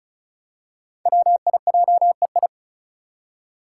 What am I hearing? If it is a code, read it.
Morse code, heard as a keyed tone, about 36 wpm, 700 Hz. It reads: WIJEI
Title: Five Challenging Letter-Number Combinations 35wpm